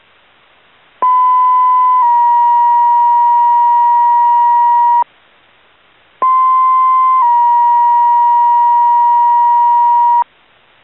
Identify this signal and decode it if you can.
Two-tone page, heard as a one-second tone, then a three-second tone, about 1010 then 950 Hz, about 1040 then 950 Hz. It is heard from 1.0 to 10.2 s.